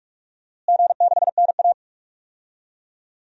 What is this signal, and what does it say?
Morse code, heard as a keyed tone, about 46 wpm, 700 Hz. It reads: G6NA
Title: DXCC-100 Call Signs - 4 Characters - Repeated 2x - 45wpm